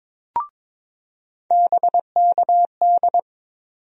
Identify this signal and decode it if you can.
Morse code, heard as a keyed tone, about 22 wpm, 700 Hz. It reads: BKD
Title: USA Call Sign Postfixes - 3 Letters 22wpm